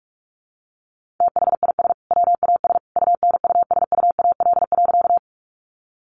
Morse code, heard as a keyed tone, about 44 wpm, 700 Hz. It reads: THIS WAS UNUSUAL.
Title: Sentences from Top 1400 Words 45wpm